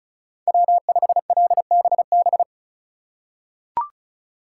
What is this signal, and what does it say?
Morse code, heard as a keyed tone, about 36 wpm, 700 Hz. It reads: W5LBB
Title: USA Call Signs - 5 Characters - 2x Character Spacing - 35wpm